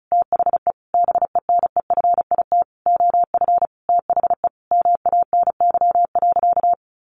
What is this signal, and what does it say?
Morse code, heard as a keyed tone, about 34 wpm, 700 Hz. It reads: THE BENEFIT OF THE MANY.
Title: Sentences from Top 1300 Words 35wpm